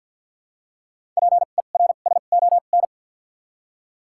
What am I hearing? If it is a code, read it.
Morse code, heard as a keyed tone, about 51 wpm, spaced as for 36 wpm, 700 Hz. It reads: PERSON
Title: 2x Character Spacing - Top 500 Words - 50wpm